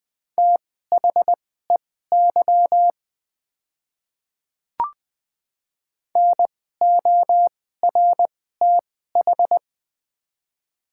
Morse code, heard as a keyed tone, about 21 wpm, spaced as for 15 wpm, 690 Hz. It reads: THEY NORTH
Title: 2x Character Spacing - Top 300 Words - 20wpm